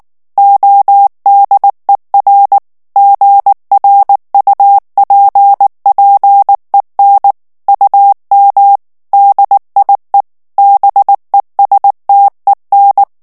Morse code, heard as a keyed tone, about 19 wpm, 800 Hz. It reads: ODER GRUPPEN UM DIE BESTEN
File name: morse mp3 / ALLEMAND texte clair 2 vitesse 18 mots minute